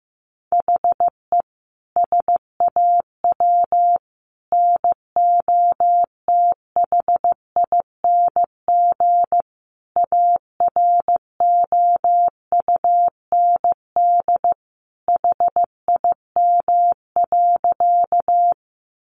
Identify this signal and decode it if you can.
Morse code, heard as a keyed tone, about 15 wpm, 700 Hz. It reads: HE SAW NOTHING AROUND HIM.
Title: Sentences from Top 600 Words 15wpm